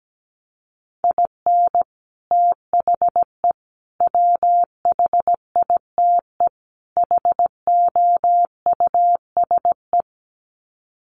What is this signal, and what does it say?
Morse code, heard as a keyed tone, about 17 wpm, 700 Hz. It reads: IN THE WHITE HOUSE
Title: Sets of 4 Words 17wpm